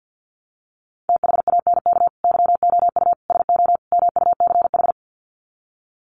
Morse code, heard as a keyed tone, about 50 wpm, 700 Hz. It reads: THANK YOU SO MUCH
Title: Sets of 4 Words 50wpm